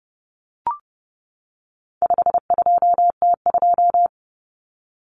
Morse code, heard as a keyed tone, about 30 wpm, 700 Hz. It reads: 52T2
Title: Common Cut Numbers - 4-Digit 30wpm